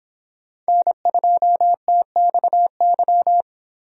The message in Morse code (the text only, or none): N2TXY